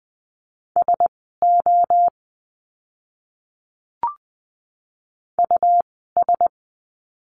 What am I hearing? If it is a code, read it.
Morse code, heard as a keyed tone, about 20 wpm, spaced as for 14 wpm, 700 Hz. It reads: SO US
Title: 139 Lesson 23f - Letter f - Words 20wpm